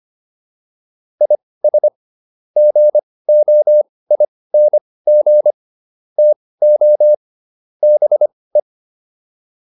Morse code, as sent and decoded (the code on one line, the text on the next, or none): .. ... / --. --- .. -. --. / - --- / -... .
IS GOING TO BE